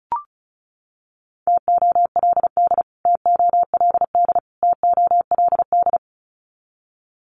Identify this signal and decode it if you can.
Morse code, heard as a keyed tone, about 34 wpm, 700 Hz. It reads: TOLD TOLD TOLD
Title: Top 500 Words Repeated 35wpm